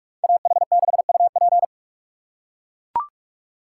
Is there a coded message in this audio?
Yes: Morse code from 0.2 to 1.6 s, heard as a keyed tone, about 41 wpm, 700 Hz. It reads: AH6UP